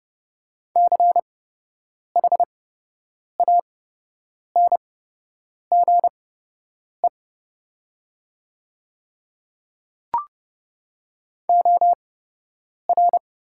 Morse code, heard as a keyed tone, about 30 wpm, spaced as for 8 wpm, 700 Hz. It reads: CHANGE OR
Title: 8x Character Spacing - Top 300 Words - 30wpm